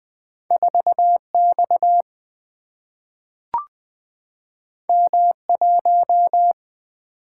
Morse code, heard as a keyed tone, about 20 wpm, 700 Hz. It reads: 4 X M 1
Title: Single-Letter Single-Number 20wpm